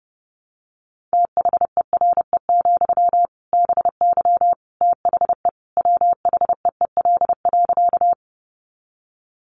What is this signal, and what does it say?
Morse code, heard as a keyed tone, about 30 wpm, 700 Hz. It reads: THERE, BY THE WHEEL.
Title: Sentences from Top 500 Words 30wpm